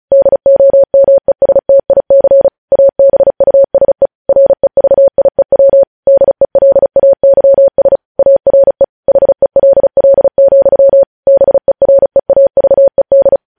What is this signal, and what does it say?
Morse code, heard as a keyed tone, about 36 wpm, 550 Hz. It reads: DOMESTIC ABUSE REVIEW DELAYS ARE HELL, BEREAVED
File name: podcast / News headlines-1x-35x35